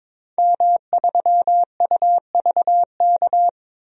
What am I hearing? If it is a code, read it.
Morse code, heard as a keyed tone, about 22 wpm, 700 Hz. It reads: M3UVK
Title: DXCC-100 Call Signs - 5 Characters - 22wpm